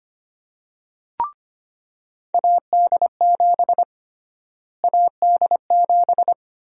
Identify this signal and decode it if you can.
Morse code, heard as a keyed tone, about 25 wpm, 700 Hz. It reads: AD7 AD7
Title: USA Call Sign Prefixes - Repeated 2x - 25wpm